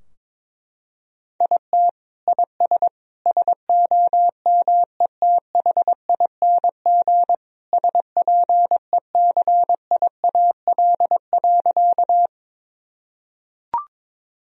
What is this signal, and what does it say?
Morse code, heard as a keyed tone, about 21 wpm, 700 Hz. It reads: IT IS SOMETHING SPECIAL.
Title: Sentences from Top 1000 Words - Review Part 3 - 22wpm